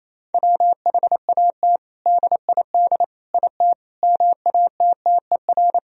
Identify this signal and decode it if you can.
Morse code, heard as a keyed tone, about 28 wpm, 700 Hz. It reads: WHAT DID IT MATTER
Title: Sentences from Top 1000 Words - Review Part 3 - 28wpm